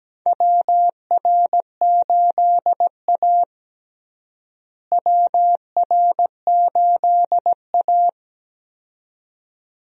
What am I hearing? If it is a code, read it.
Morse code, heard as a keyed tone, about 17 wpm, 700 Hz. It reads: WR8A WR8A
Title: USA Call Signs - 4 Characters - Repeated 2x - 17wpm